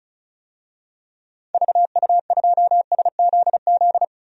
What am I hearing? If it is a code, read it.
Morse code, heard as a keyed tone, about 34 wpm, 700 Hz. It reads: VU2SZZ
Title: DXCC-100 Call Signs - 6 Characters - 35wpm